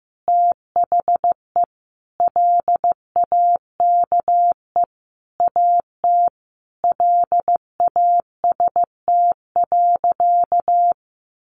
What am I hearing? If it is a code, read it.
Morse code, heard as a keyed tone, about 15 wpm, 700 Hz. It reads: THE LAKE AT LAST.
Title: Sentences from Top 1100 Words 15wpm